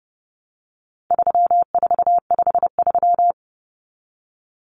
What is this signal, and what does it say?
Morse code, heard as a keyed tone, about 30 wpm, 700 Hz. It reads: 3453
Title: Numbers - 4-Digit 30wpm